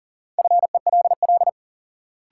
Morse code, heard as a keyed tone, about 40 wpm, 700 Hz. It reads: FELL
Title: Sentences from Top 1600 Words - New Word Drill - 40wpm